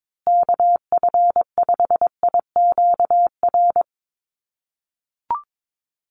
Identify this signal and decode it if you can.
Morse code, heard as a keyed tone, about 22 wpm, 700 Hz. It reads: KF5IQR
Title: USA Call Signs Repeated 3x 22wpm